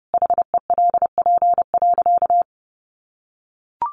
Morse code, heard as a keyed tone, about 30 wpm, 700 Hz. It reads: HELP.
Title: Sentences from Top 700 Words 30wpm